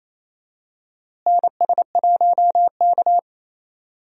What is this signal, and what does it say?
Morse code, heard as a keyed tone, about 28 wpm, 700 Hz. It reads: NS1K
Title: USA Call Signs - 4 Characters - Repeated 3x - 28wpm